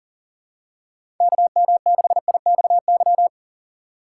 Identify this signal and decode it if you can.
Morse code, heard as a keyed tone, about 40 wpm, 700 Hz. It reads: KM6IXY
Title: USA Call Signs - 6 Characters - 2x Character Spacing - 40wpm